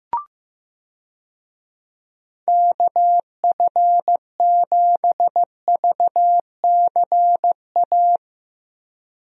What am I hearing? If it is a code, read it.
Morse code, heard as a keyed tone, about 16 wpm, 700 Hz. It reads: KF7VCA
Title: USA Call Signs - 6 Characters - 15wpm